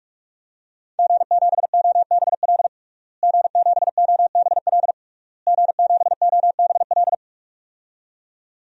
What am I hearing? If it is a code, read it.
Morse code, heard as a keyed tone, about 46 wpm, 700 Hz. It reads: G7OBL G7OBL G7OBL
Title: DXCC-100 Call Signs - 5 Characters - Repeated 3x - 45wpm